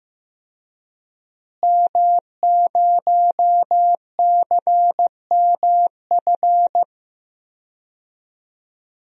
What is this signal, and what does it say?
Morse code, heard as a keyed tone, about 15 wpm, 700 Hz. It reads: M0CMF
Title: DXCC-100 Call Signs - 5 Characters - Repeated 2x - 15wpm